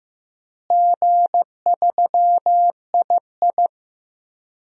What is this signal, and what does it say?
Morse code, heard as a keyed tone, about 15 wpm, 700 Hz. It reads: G3II
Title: DXCC-100 Call Signs - 4 Characters - Repeated 3x - 15wpm